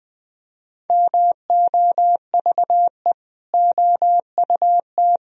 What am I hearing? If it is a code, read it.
Morse code, heard as a keyed tone, about 20 wpm, 700 Hz. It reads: MOVE OUT
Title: Sentences from Top 1000 Words - Review Part 1 - 20wpm